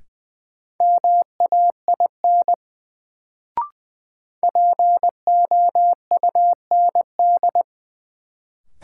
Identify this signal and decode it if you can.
Morse code, heard as a keyed tone, about 21 wpm, 700 Hz. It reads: MAIN POUND
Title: Sentences from Top 1000 Words - Review Part 2 - New Word Drill - 20wpm